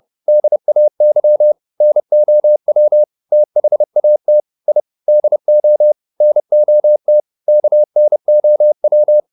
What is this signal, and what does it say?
Morse code, heard as a keyed tone, about 30 wpm, 600 Hz. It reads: DAY NOW THAT I DO NOT KNOW
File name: the old man and the sea / the old man and the sea 1 30wpm